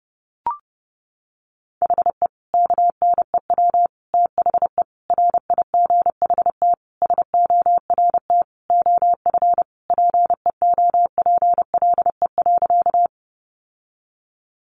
Morse code, heard as a keyed tone, about 30 wpm, 700 Hz. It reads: HE KNEW THE RIGHT SORT OF PEOPLE.
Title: Sentences from Top 1300 Words 30wpm